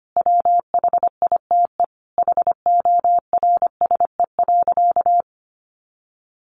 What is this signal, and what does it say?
Morse code, heard as a keyed tone, about 25 wpm, 700 Hz. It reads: WHITE HORSE.
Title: Sentences from Top 300 Words 25wpm